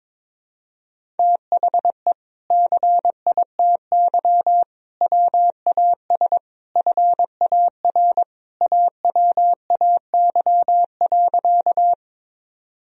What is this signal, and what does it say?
Morse code, heard as a keyed tone, about 22 wpm, 700 Hz. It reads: THE CITY WAS FAR AWAY.
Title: Sentences from Top 1000 Words - Review Part 1 - 22wpm